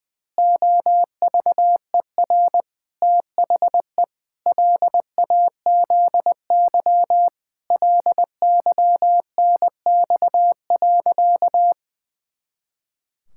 Morse code, heard as a keyed tone, about 21 wpm, 700 Hz. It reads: OVER THE LAZY LYNX.
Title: Pangram Challenge v2 - 20wpm